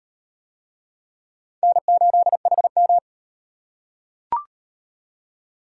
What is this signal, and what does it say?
Morse code, heard as a keyed tone, about 37 wpm, 700 Hz. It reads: N8HM